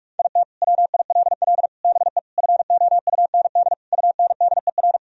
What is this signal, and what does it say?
Morse code, heard as a keyed tone, about 46 wpm, 700 Hz. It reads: IT WILL BE FOUND UNDER
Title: Sentences from Top 1000 Words - Review Part 1 - 45wpm